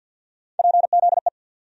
Morse code, heard as a keyed tone, about 53 wpm, 700 Hz. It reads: PZE
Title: USA Call Sign Postfixes - 3 Letters 50wpm